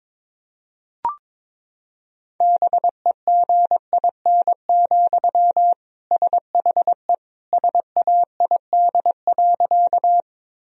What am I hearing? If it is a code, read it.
Morse code, heard as a keyed tone, about 22 wpm, 700 Hz. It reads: BEGIN, SHE SAID.
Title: Sentences from Top 1000 Words - Review Part 1 - 22wpm